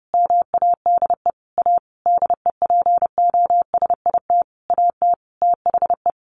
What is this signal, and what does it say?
Morse code, heard as a keyed tone, about 30 wpm, 700 Hz. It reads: MADE A DEPOSIT AT THE
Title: Sentences from Top 1400 Words 30wpm